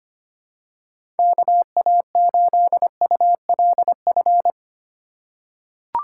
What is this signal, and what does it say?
Morse code, heard as a keyed tone, about 25 wpm, 700 Hz. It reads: KA8ULF
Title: USA Call Signs - 6 Characters - 4x Character Spacing - 25wpm